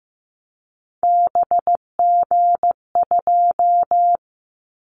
Morse code, heard as a keyed tone, about 15 wpm, 700 Hz. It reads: BG2
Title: Callsign Prefixes for Top 100 DXCC - 15wpm